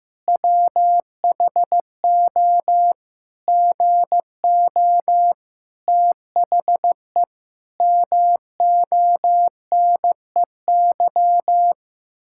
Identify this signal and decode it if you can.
Morse code, heard as a keyed tone, about 15 wpm, 700 Hz. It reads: WHO GO THE MONEY
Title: Pangram Challenge v2 - 15wpm